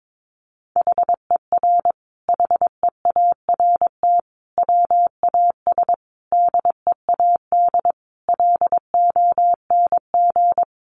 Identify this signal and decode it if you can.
Morse code, heard as a keyed tone, about 22 wpm, 700 Hz. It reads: HER HEART WAS DEAD LONG
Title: Sentences from Top 1500 Words 22wpm